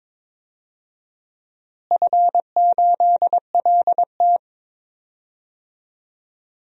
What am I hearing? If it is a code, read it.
Morse code, heard as a keyed tone, about 22 wpm, 700 Hz. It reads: F8LT
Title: DXCC-100 Call Signs - 4 Characters - Repeated 2x - 22wpm